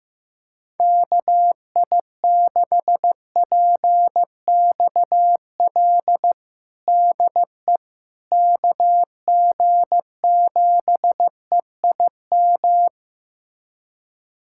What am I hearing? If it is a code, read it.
Morse code, heard as a keyed tone, about 16 wpm, 700 Hz. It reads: KI6PXL DE KG7EIM
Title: USA Callsign DE Callsign 15wpm